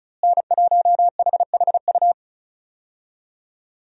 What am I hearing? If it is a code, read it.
Morse code, heard as a keyed tone, about 36 wpm, 700 Hz. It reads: N1HHU
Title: USA Call Signs - 5 Characters - Repeated 3x - 35wpm